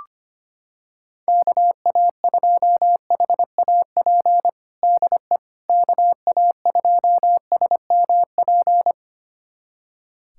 Morse code, heard as a keyed tone, about 25 wpm, 700 Hz. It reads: KA2HAP DE KA2SMP
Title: USA Callsign DE Callsign - 6 Characters - 25wpm